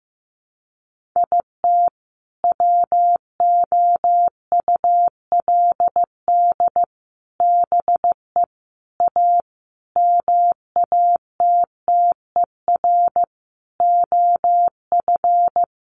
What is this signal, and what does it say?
Morse code, heard as a keyed tone, about 15 wpm, 700 Hz. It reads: IT WOULD BE A MATTER OF
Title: Sentences from Top 1400 Words 15wpm